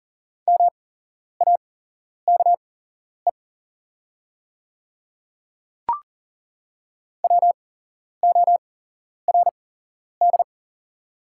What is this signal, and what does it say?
Morse code, heard as a keyed tone, about 40 wpm, spaced as for 11 wpm, 700 Hz. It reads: MAKE WORD
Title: 8x Character Spacing - Top 100 Words - 40wpm